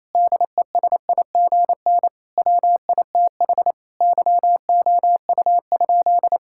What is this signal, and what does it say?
Morse code, heard as a keyed tone, about 28 wpm, 700 Hz. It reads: DESIGN WITH YOU?
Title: Sentences from Top 1000 Words - Review Part 5 - 28wpm